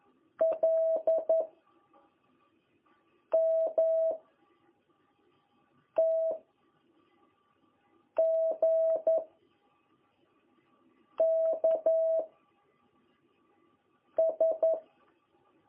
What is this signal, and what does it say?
Morse code, heard as a keyed tone, about 11 wpm, 660 Hz. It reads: L M T G K S
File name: jh8cbh / 2021cw